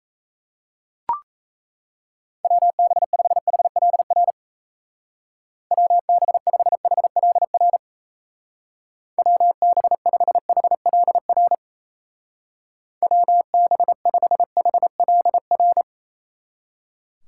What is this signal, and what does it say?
Morse code, heard as a keyed tone, about 34 wpm, 700 Hz. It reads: WB5HLR WB5HLR WB5HLR WB5HLR
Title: Speed Racing - USA 2x3 Call Signs 28wpm